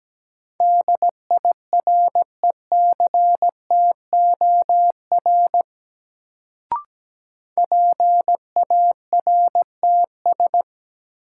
Morse code, heard as a keyed tone, about 17 wpm, 700 Hz. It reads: DIRECTOR PARTS
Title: Sets of 3 Words - New Word Drill - 17wpm